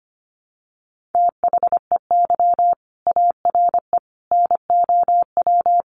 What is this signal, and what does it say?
Morse code, heard as a keyed tone, about 25 wpm, 700 Hz. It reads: THEY ARE NOW
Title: Sentences from Top 900 Words 25wpm